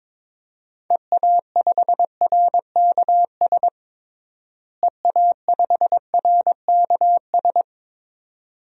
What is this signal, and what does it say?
Morse code, heard as a keyed tone, about 22 wpm, 700 Hz. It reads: EA5RKS EA5RKS
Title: DXCC-100 Call Signs - 6 Characters - Repeated 2x - 22wpm